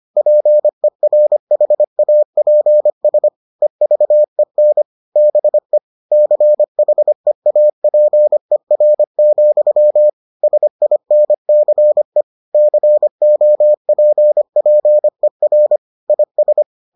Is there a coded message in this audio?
Yes: Morse code from 0.2 to 16.6 s, heard as a keyed tone, about 25 wpm, 600 Hz. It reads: PERHAPS EVEN BE CHEAPER, SINCE COPPER IS